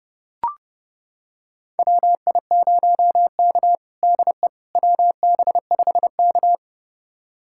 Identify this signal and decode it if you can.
Morse code, heard as a keyed tone, about 32 wpm, 700 Hz. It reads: WI0K DE WB5K
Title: USA Callsign DE Callsign - 4 Characters - 30wpm